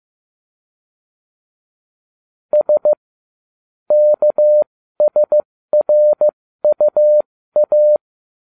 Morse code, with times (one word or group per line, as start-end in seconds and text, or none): 2.5-2.9 S
3.9-8.0 KSRUA